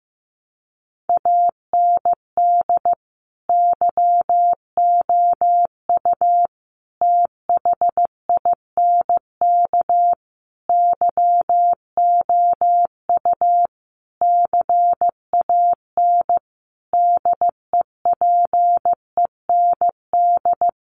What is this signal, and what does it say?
Morse code, heard as a keyed tone, about 15 wpm, 700 Hz. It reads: AND YOU THINK YOU CAN DEPEND
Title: Sentences from Top 1300 Words 15wpm